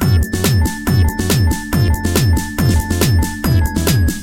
There is no code.